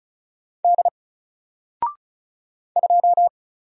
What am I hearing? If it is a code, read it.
Morse code, heard as a keyed tone, about 36 wpm, 700 Hz. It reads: D 2
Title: Single Letter-Number 35wpm